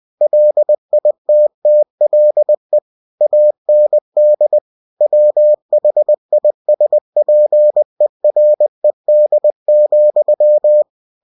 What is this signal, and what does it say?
Morse code, heard as a keyed tone, about 20 wpm, 600 Hz. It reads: LITTLE AND WHISPERED,
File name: cw-books / winnie-the-pooh / winnie-20wpm-page-0107